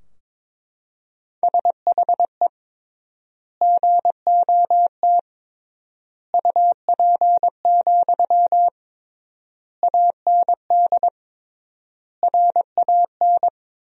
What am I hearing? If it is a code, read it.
Morse code, heard as a keyed tone, about 22 wpm, 700 Hz. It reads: SHE GOT UP, AND RAN
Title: 3x Word-Spacing - Sentences from Top 500 Words - 22wpm